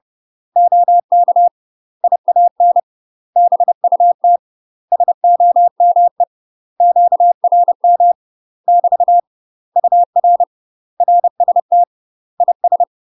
Morse code, heard as a keyed tone, about 30 wpm, 700 Hz. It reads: OK IAN BUT SOME QRM <BT> UR RST IS
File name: UK MASTER QSOs TRAINING / 30WPM / UK MASTER QSOs 0007